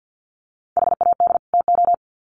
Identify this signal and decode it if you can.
Morse code, heard as a keyed tone, about 50 wpm, 700 Hz. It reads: HAD TO
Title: Sets of 2 Words 50wpm